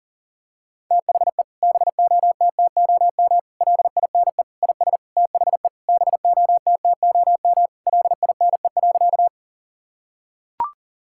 Morse code, heard as a keyed tone, about 40 wpm, 700 Hz. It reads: THE BOTTOM LINE IS THE BOTTOM LINE.
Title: Sentences from Top 1000 Words - Review Part 6 - 40wpm